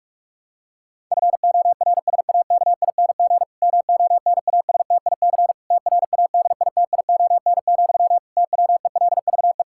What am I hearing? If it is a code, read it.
Morse code, heard as a keyed tone, about 46 wpm, 700 Hz. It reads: FORSAKING MONASTIC TRADITION, TWELVE
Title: Pangram Challenge v2 - 45wpm